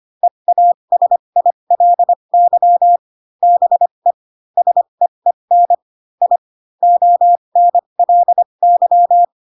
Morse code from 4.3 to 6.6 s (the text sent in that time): SEEN I